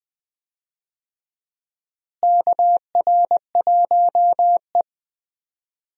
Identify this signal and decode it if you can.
Morse code, heard as a keyed tone, about 20 wpm, 700 Hz. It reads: KR1E
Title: USA Call Signs - 4 Characters - Repeated 2x - 20wpm